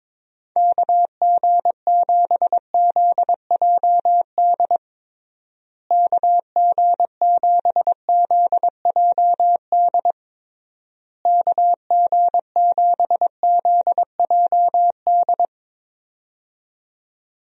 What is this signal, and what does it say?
Morse code, heard as a keyed tone, about 22 wpm, 700 Hz. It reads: KG7ZJD KG7ZJD KG7ZJD
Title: USA Call Signs - 6 Characters - Repeated 3x - 22wpm